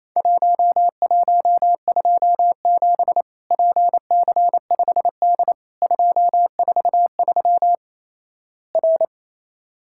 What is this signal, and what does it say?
Morse code, heard as a keyed tone, about 28 wpm, 700 Hz. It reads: 1127 PC5D 243 R
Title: Worked All Europe DX Contest - QTC Exchange - 28wpm